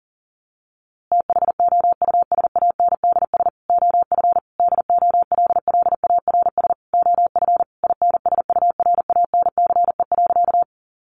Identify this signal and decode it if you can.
Morse code, heard as a keyed tone, about 40 wpm, 700 Hz. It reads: THOUSANDS OF DOLLARS OF INSURANCE.
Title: A Sound Of Thunder 40wpm